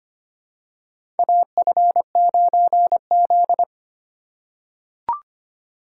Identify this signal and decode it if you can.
Morse code, heard as a keyed tone, about 25 wpm, 700 Hz. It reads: AF9Z
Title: USA Call Signs - 4 Characters - 2x Character Spacing - 25wpm